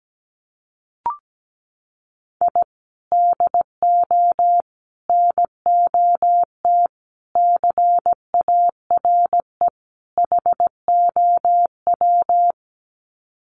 Morse code, heard as a keyed tone, about 17 wpm, 700 Hz. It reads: I DO NOT CARE HOW
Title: Sets of 5 Words Encore2 17wpm